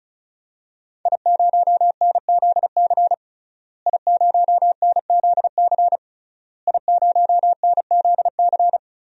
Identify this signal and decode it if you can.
Morse code, heard as a keyed tone, about 36 wpm, 700 Hz. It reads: I0NZC I0NZC I0NZC
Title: DXCC-100 Call Signs - 5 Characters - Repeated 3x - 35wpm